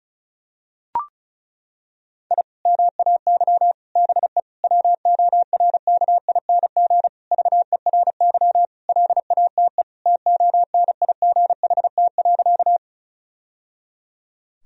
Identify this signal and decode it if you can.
Morse code, heard as a keyed tone, about 36 wpm, 700 Hz. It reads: I MAY BE WORKING VERY LATE TONIGHT.
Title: Sentences from Top 1000 Words - Review Part 4 - 35wpm